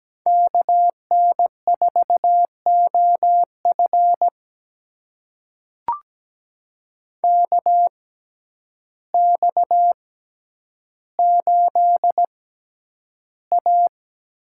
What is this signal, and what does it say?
Morse code, heard as a keyed tone, about 17 wpm, spaced as for 9 wpm, 700 Hz. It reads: KN4OF KX8A
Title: USA Call Signs - 6x Character Spacing - 17wpm